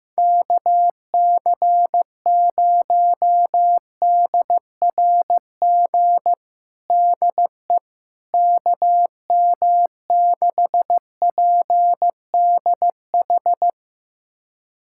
Morse code, heard as a keyed tone, about 15 wpm, 700 Hz. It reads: KC0DRG DE KM6PDH
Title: USA Callsign DE Callsign - 6 Characters - 15wpm